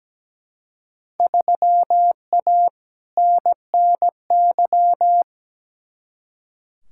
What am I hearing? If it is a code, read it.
Morse code, heard as a keyed tone, about 17 wpm, 700 Hz. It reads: 3A NNY
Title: Field Day Exchanges - 17wpm